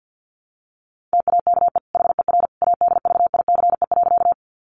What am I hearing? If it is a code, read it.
Morse code, heard as a keyed tone, about 50 wpm, 700 Hz. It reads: TAKE HER ADVICE.
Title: Sentences from Top 1100 Words 50wpm